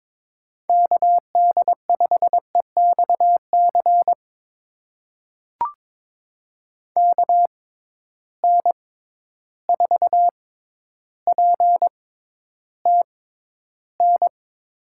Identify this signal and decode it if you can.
Morse code, heard as a keyed tone, about 22 wpm, spaced as for 11 wpm, 700 Hz. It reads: KD5EXC KN4PTN
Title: USA Call Signs - 6 Characters - 6x Character Spacing - 22wpm